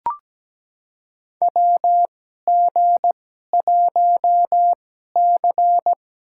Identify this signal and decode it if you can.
Morse code, heard as a keyed tone, about 17 wpm, spaced as for 12 wpm, 700 Hz. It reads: WG1C
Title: USA Call Signs - 2x Character Spacing - 17wpm